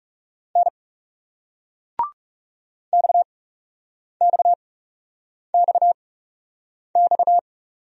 Morse code, heard as a keyed tone, about 40 wpm, spaced as for 8 wpm, 700 Hz. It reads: N XXXX